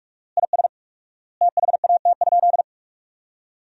Morse code, heard as a keyed tone, about 44 wpm, 700 Hz. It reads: IS THAT?
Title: 4x Word-Spacing - Sentences from Top 100 Words - 45wpm